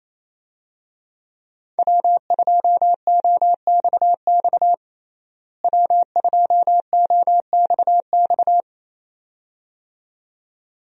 Morse code, heard as a keyed tone, about 28 wpm, 700 Hz. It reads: W2OXX W2OXX
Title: USA Call Signs - 5 Characters - Repeated 2x - 28wpm